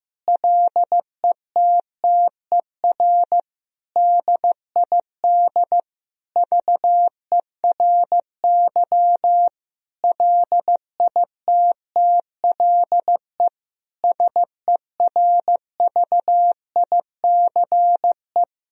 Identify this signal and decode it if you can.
Morse code, heard as a keyed tone, about 15 wpm, 700 Hz. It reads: LETTER DID VERY LITTLE SERVICE
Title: Sentences from Top 1000 Words - Review Part 4 - 15wpm